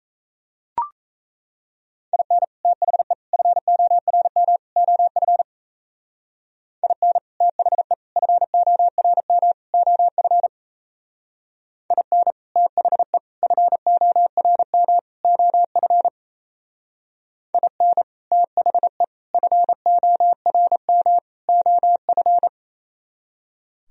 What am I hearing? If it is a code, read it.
Morse code, heard as a keyed tone, about 34 wpm, 700 Hz. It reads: IN THE FORM OF IN THE FORM OF IN THE FORM OF IN THE FORM OF
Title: Speed-Racing - Sets of 4 Words 28wpm